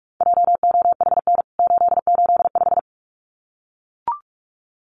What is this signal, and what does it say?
Morse code, heard as a keyed tone, about 44 wpm, 700 Hz. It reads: JOHN 885
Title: CWT Exchange 45wpm